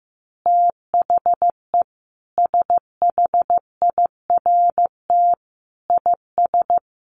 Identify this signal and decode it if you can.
Morse code, heard as a keyed tone, about 15 wpm, 700 Hz. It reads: THE SHIRT IS
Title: Additional Easy Phrases 15wpm